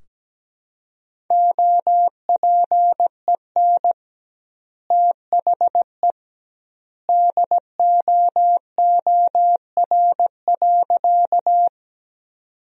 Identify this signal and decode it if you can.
Morse code, heard as a keyed tone, about 17 wpm, 700 Hz. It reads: OPEN THE DOOR.
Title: 2x Word-Spacing - Sentences from Top 300 Words - 17wpm